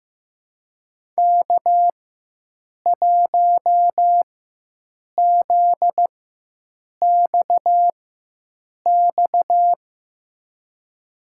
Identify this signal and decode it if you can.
Morse code, heard as a keyed tone, about 16 wpm, spaced as for 7 wpm, 700 Hz. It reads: K1ZXX
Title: USA Call Signs - 5 Characters - 4x Character Spacing - 15wpm